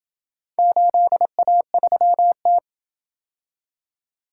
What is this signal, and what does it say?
Morse code, heard as a keyed tone, about 28 wpm, 700 Hz. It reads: 8A3T